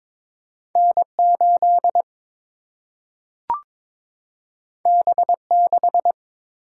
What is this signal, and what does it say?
Morse code, heard as a keyed tone, about 22 wpm, 700 Hz. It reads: N8 B6
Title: Single-Letter Single-Number 22wpm